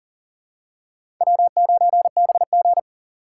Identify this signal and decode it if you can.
Morse code, heard as a keyed tone, about 40 wpm, 700 Hz. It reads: W9BG
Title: USA Call Signs - 4 Characters - 2x Character Spacing - 40wpm